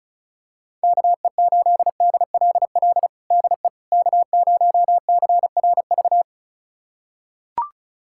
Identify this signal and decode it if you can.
Morse code, heard as a keyed tone, about 34 wpm, 700 Hz. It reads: KE8DLL DE K0CRV
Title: USA Callsign DE Callsign 35wpm